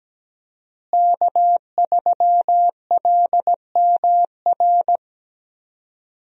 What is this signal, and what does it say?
Morse code, heard as a keyed tone, about 17 wpm, 700 Hz. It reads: K3LMR
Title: USA Call Signs - 5 Characters - Repeated 2x - 17wpm